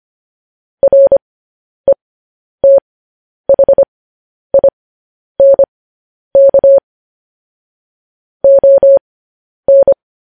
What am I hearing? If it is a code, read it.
Morse code, heard as a keyed tone, about 25 wpm, spaced as for 10 wpm, 550 Hz. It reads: RETHINK ON
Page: news-headlines-in-morse-code-at-10-wpm